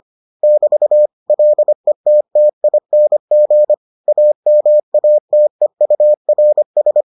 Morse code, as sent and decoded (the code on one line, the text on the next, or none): -...- / .-.. - - .. -. --. / .- -- .- - . ..- .-. ...
<BT> LETTING AMATEURS